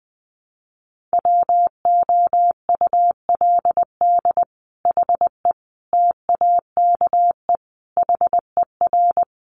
Morse code, heard as a keyed tone, about 20 wpm, 700 Hz. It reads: WOULD HE TAKE HER
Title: Sentences from Top 1500 Words 20wpm